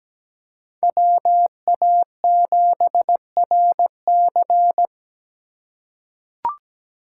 Morse code, heard as a keyed tone, about 17 wpm, 700 Hz. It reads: WA7RC
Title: USA Call Signs - 5 Characters - 4x Character Spacing - 17wpm